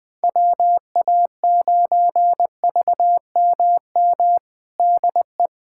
Morse code, heard as a keyed tone, about 20 wpm, 700 Hz. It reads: WA9VMM DE
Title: USA Callsign DE Callsign - 6 Characters - 20wpm